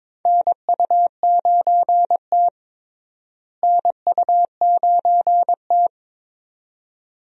Morse code, heard as a keyed tone, about 21 wpm, 700 Hz. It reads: NU9T NU9T
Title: USA Call Signs - 4 Characters - Repeated 2x - 22wpm